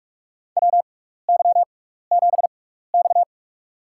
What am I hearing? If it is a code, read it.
Morse code, heard as a keyed tone, about 46 wpm, spaced as for 16 wpm, 700 Hz. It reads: WY7X
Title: USA Call Signs - 4 Characters - 6x Character Spacing - 45wpm